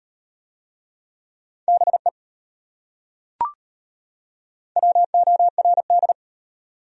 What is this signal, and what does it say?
Morse code, heard as a keyed tone, about 34 wpm, 700 Hz. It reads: BE WORD